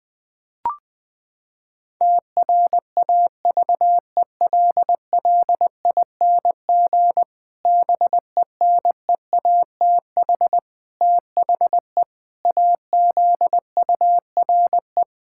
Morse code, heard as a keyed tone, about 21 wpm, 700 Hz. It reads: TRAVELLING BENEATH THE AZURE
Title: Pangram Challenge v2 - 20wpm